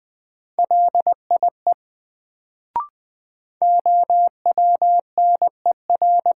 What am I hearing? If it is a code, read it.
Morse code, heard as a keyed tone, about 20 wpm, 700 Hz. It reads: LIE OWNER
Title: Sentences from Top 1600 Words - New Word Drill - 20wpm